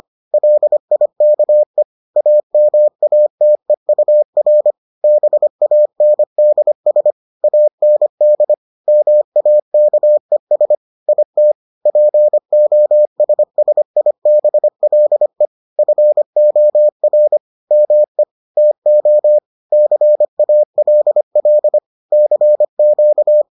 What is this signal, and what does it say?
Morse code, heard as a keyed tone, about 25 wpm, 600 Hz. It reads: LIKE AMATEUR BANDS AND MAKES IT POSSIBLE FOR ME TO CALL CQ